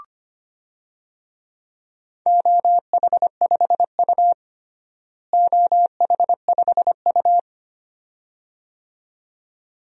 Morse code, heard as a keyed tone, about 25 wpm, 700 Hz. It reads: OH5U OH5U
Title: DXCC-100 Call Signs - 4 Characters - Repeated 2x - 25wpm